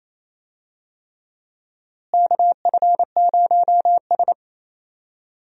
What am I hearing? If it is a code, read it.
Morse code, heard as a keyed tone, about 27 wpm, 700 Hz. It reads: KF0S
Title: USA Call Signs - 4 Characters - Repeated 3x - 28wpm